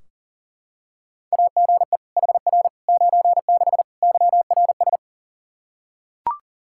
Morse code, heard as a keyed tone, about 40 wpm, 700 Hz. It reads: AGE HR 96 YRS
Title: K1USN SST QSO Practice 40wpm